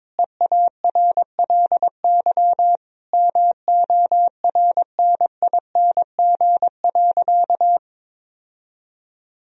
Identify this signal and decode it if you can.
Morse code, heard as a keyed tone, about 22 wpm, 700 Hz. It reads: EARLY MORNING.
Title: Sentences from Top 1000 Words - Review Part 3 - 22wpm